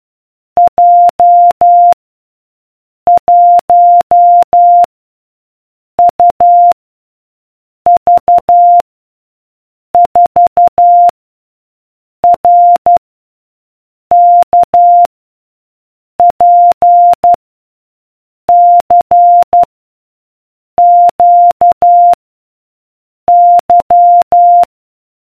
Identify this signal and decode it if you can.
Morse code, heard as a keyed tone, about 12 wpm, 700 Hz. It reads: J 1 U V 4 R K P C Q Y